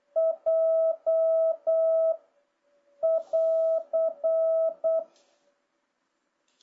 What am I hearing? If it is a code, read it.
Morse code, heard as a keyed tone, about 8 wpm, 650 Hz. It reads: J <AR>
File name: jh8cbh / 2026cw